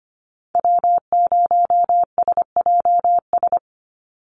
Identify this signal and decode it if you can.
Morse code, heard as a keyed tone, about 25 wpm, 700 Hz. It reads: W0SJS
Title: Call Signs Repeated 3x - 25wpm